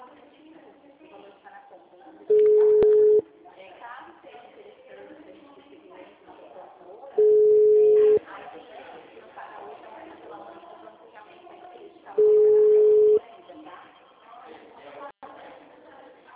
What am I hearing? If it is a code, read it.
Ringback tone (Europe, 425 Hz), with speech.